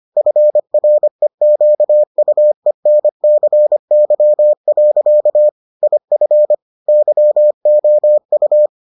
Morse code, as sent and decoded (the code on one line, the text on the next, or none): ..-. .-. --.- ..- . -. -.-. -.-- .-.-.- / .. ..-. / -.-- --- ..-
FREQUENCY. IF YOU